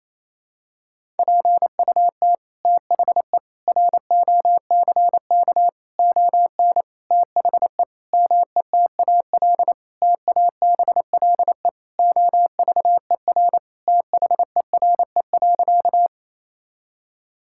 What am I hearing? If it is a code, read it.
Morse code, heard as a keyed tone, about 28 wpm, 700 Hz. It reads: PUT THE ROCK ON THE METAL TABLE OVER THERE.
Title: Sentences from Top 1000 Words - Review Part 6 - 28wpm